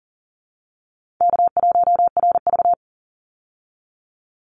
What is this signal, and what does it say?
Morse code, heard as a keyed tone, about 39 wpm, 700 Hz. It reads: K1RV
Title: Easy Call Signs Repeated 3x - 40wpm